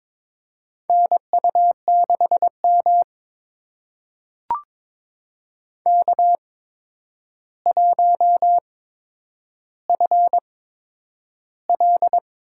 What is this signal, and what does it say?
Morse code, heard as a keyed tone, about 22 wpm, spaced as for 10 wpm, 700 Hz. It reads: NU6M K1FL
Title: USA Call Signs - 4 Characters - 8x Character Spacing - 22wpm